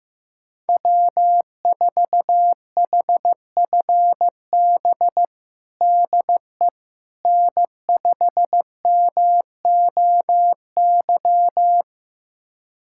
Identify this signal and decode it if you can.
Morse code, heard as a keyed tone, about 15 wpm, 700 Hz. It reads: W4HFB DE N5MOY
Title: USA Callsign DE Callsign - 5 Characters - 15wpm